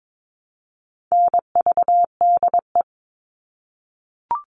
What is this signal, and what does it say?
Morse code, heard as a keyed tone, about 22 wpm, 700 Hz. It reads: NVDE
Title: All Cut Numbers - 4-Digit 22wpm